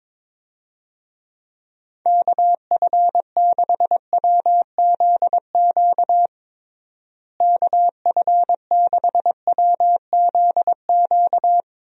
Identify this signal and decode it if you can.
Morse code, heard as a keyed tone, about 22 wpm, 700 Hz. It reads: KF6WZQ KF6WZQ
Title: USA Call Signs - 6 Characters - Repeated 3x - 22wpm